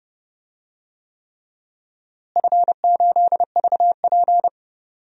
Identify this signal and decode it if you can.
Morse code, heard as a keyed tone, about 30 wpm, 700 Hz. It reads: F8VP
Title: DXCC-100 Call Signs - 4 Characters - Repeated 3x - 30wpm